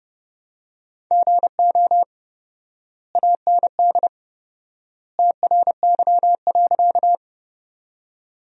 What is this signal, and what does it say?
Morse code, heard as a keyed tone, about 32 wpm, 700 Hz. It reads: GO AND TRY.
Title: 4x Word-Spacing - Sentences from Top 200 Words - 30wpm